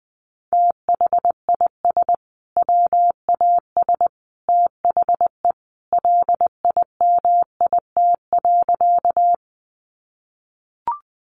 Morse code, heard as a keyed tone, about 20 wpm, 700 Hz. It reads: THIS WAS THE LIMIT.
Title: Sentences from Top 1500 Words 20wpm